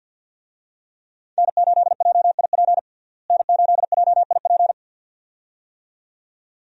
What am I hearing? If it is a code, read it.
Morse code, heard as a keyed tone, about 52 wpm, 700 Hz. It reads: N8JIP N8JIP
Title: USA Call Signs - 5 Characters - Repeated 2x - 50wpm